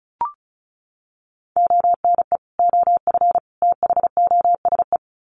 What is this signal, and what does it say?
Morse code, heard as a keyed tone, about 34 wpm, 700 Hz. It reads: ONE OF THOSE
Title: Sets of 3 Words 35wpm